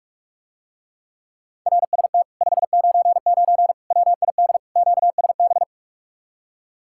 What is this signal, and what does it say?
Morse code, heard as a keyed tone, about 44 wpm, 700 Hz. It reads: RST 599 WID QSB